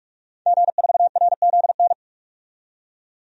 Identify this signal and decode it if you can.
Morse code, heard as a keyed tone, about 46 wpm, 700 Hz. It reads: G4RZN
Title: DXCC-100 Call Signs - 5 Characters - 45wpm